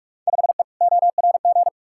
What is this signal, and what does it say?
Morse code, heard as a keyed tone, about 46 wpm, 700 Hz. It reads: 5E ORG
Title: Field Day Exchanges Encore - 45wpm